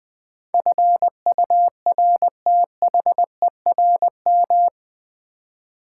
Morse code, heard as a keyed tone, about 21 wpm, 690 Hz. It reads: FURTHERM
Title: Build A Word - 4 Letters At A Time - 11-Letter Words - 20wpm